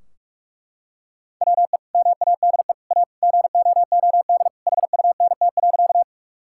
Morse code, heard as a keyed tone, about 41 wpm, 700 Hz. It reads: WE MADE A GOOD HUNT.
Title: Sentences from Top 1600 Words 45wpm